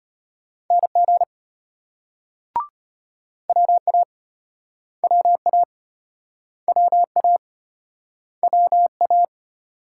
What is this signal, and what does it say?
Morse code, heard as a keyed tone, about 32 wpm, 700 Hz. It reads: NG WA WA WA WA